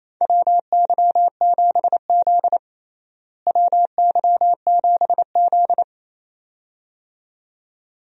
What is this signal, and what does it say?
Morse code, heard as a keyed tone, about 28 wpm, 700 Hz. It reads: WY7Z WY7Z
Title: USA Call Signs - 4 Characters - Repeated 2x - 28wpm